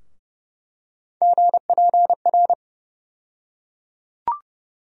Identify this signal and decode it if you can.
Morse code, heard as a keyed tone, about 30 wpm, 700 Hz. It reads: GPR
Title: USA Call Sign Postfixes - 3 Letters 30wpm